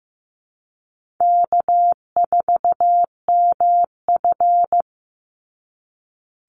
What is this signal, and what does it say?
Morse code, heard as a keyed tone, about 15 wpm, 700 Hz. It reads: K4MF
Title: NA SOTA Chaser Callsigns 15wpm